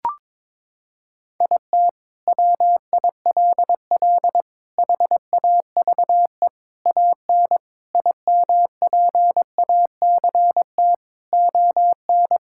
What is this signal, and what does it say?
Morse code, heard as a keyed tone, about 22 wpm, 700 Hz. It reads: IT WILL HAVE AN IMPACT ON
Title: Sentences from Top 1000 Words - Review Part 6 - 22wpm